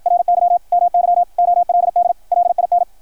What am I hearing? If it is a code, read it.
Morse code, heard as a keyed tone, about 53 wpm, 690 Hz, with radio noise. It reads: RY MY OLD LIN
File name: archived / CRSnet